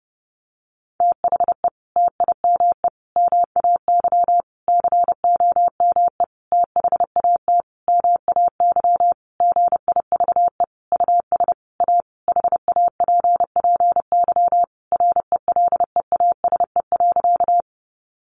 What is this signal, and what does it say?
Morse code, heard as a keyed tone, about 30 wpm, 700 Hz. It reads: THE TIME MAY COME THAT MAY GIVE US A HAPPY RELEASE.
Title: Sentences from Top 1100 Words 30wpm